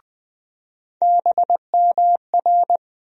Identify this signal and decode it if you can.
Morse code, heard as a keyed tone, about 21 wpm, 700 Hz. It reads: BMR
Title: USA Call Sign Postfixes - 3 Letters 20wpm